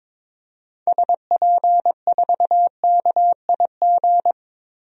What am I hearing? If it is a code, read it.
Morse code, heard as a keyed tone, about 22 wpm, 700 Hz. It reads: SP4KIG
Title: DXCC-100 Call Signs - 6 Characters - 22wpm